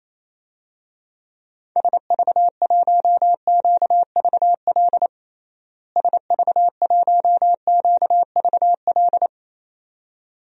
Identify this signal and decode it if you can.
Morse code, heard as a keyed tone, about 28 wpm, 700 Hz. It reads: SV1QVL SV1QVL
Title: DXCC-100 Call Signs - 6 Characters - Repeated 2x - 28wpm